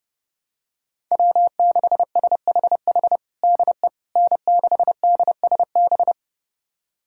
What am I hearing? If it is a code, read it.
Morse code, heard as a keyed tone, about 32 wpm, 700 Hz. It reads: W6SHH DE N6DSB
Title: USA Callsign DE Callsign - 5 Characters - 30wpm